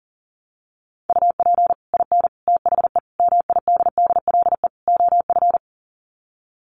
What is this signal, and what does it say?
Morse code, heard as a keyed tone, about 40 wpm, 700 Hz. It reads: UP IN THE MIDDLE OF
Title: Sets of 5 Words Encore 40wpm